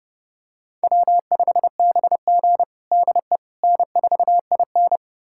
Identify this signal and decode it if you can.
Morse code, heard as a keyed tone, about 30 wpm, 700 Hz. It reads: W5BG DE N4IN
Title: USA Callsign DE Callsign - 4 Characters - 30wpm